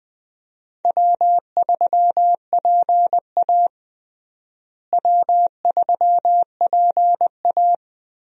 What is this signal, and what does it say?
Morse code, heard as a keyed tone, about 21 wpm, 700 Hz. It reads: W3PA W3PA
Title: USA Call Signs - 4 Characters - Repeated 3x - 20wpm